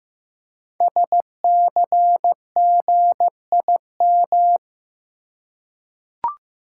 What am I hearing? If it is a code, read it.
Morse code, heard as a keyed tone, about 15 wpm, 700 Hz. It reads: SCGIM
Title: Code Groups - 5 Letter:Numbers - 15wpm